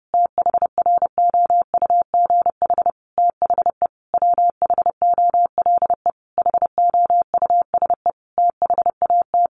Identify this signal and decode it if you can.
Morse code, heard as a keyed tone, about 30 wpm, 700 Hz. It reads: THROUGH THE WHOLE HOUSE THAT
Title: Sentences from Top 1200 Words 30wpm